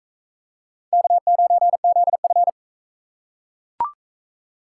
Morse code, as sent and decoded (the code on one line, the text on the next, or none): -.- ----. --.. ..-.
K9ZF